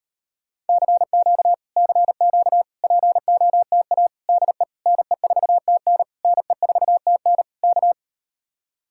Morse code, heard as a keyed tone, about 38 wpm, 700 Hz. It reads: CQ CQ POTA DE NE4TN NE4TN K